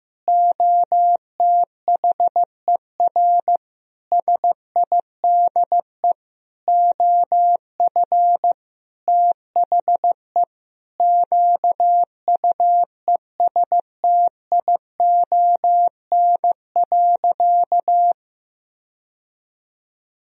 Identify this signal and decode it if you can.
Morse code, heard as a keyed tone, about 15 wpm, 700 Hz. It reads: OTHER SIDE OF THE QUESTION.
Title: Sentences from Top 1000 Words - Review Part 1 - 15wpm